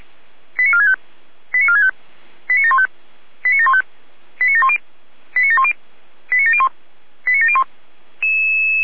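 Five-tone selective call (ZVEI), heard as quick runs of five tones, tones of about 70 ms. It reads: 89465 89465 89715 89715 89710 89710 89701 89701
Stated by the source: Received on an SDR, ZVEI five-tone call.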